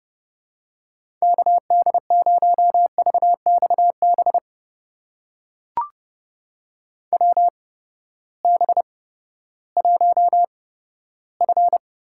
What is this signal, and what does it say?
Morse code, heard as a keyed tone, about 32 wpm, spaced as for 15 wpm, 700 Hz. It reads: KD0VXB WB1F
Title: USA Call Signs - 6 Characters - 8x Character Spacing - 30wpm